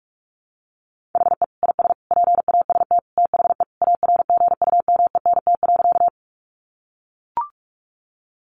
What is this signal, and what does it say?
Morse code, heard as a keyed tone, about 44 wpm, 700 Hz. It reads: HE IS PAST THE ARGUMENT.
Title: Sentences from Top 1400 Words 45wpm